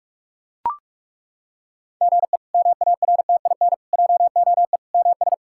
Morse code, heard as a keyed tone, about 41 wpm, 700 Hz. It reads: GE MARTIN JOE MS
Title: K1USN SST QSO Practice - 45wpm